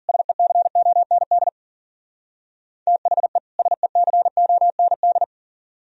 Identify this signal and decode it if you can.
Morse code, heard as a keyed tone, about 52 wpm, 700 Hz. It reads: SECOND THE SECOND